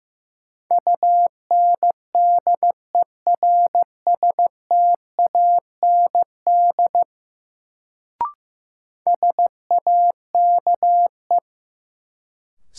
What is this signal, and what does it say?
Morse code, heard as a keyed tone, about 16 wpm, 700 Hz. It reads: UNDERSTAND SAKE
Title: Sets of 4 Words - New Word Drill - 15wpm